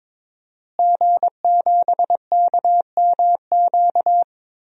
Morse code, heard as a keyed tone, about 22 wpm, 700 Hz. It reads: G7KMQ
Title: DXCC-100 Call Signs - 5 Characters - Repeated 2x - 22wpm